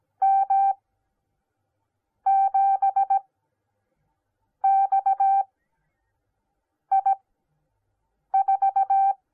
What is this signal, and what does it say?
Morse code, heard as a keyed tone, about 17 wpm, spaced as for 5 wpm, 780 Hz. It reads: M7XI4